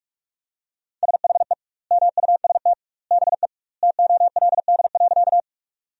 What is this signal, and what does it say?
Morse code, heard as a keyed tone, about 48 wpm, 700 Hz. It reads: SHE MUST BE TOLD.
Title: 2x Word-Spacing - Sentences from Top 400 Words - 45wpm